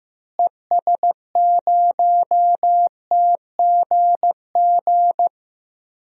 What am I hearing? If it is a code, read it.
Morse code, heard as a keyed tone, about 16 wpm, 700 Hz. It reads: ES0TGG
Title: DXCC-100 Call Signs - 6 Characters - Repeated 3x - 15wpm